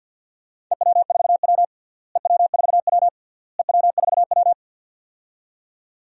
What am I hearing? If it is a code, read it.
Morse code, heard as a keyed tone, about 47 wpm, 700 Hz. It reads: EW4W EW4W EW4W
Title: DXCC-100 Call Signs - 4 Characters - Repeated 3x - 50wpm